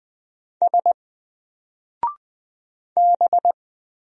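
Morse code, heard as a keyed tone, about 20 wpm, 700 Hz. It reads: S B
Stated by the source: Single letter and number Morse drill at 20 wpm, from N4LCD.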